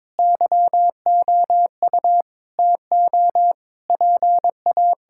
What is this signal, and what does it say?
Morse code, heard as a keyed tone, about 22 wpm, 700 Hz. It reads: YOU TO PA
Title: Sentences from Top 1000 Words - Review Part 6 - 22wpm